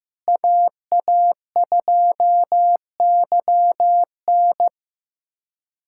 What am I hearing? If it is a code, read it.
Morse code, heard as a keyed tone, about 15 wpm, 700 Hz. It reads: AA2YN
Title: USA Callsign DE Callsign - 5 Characters - 15wpm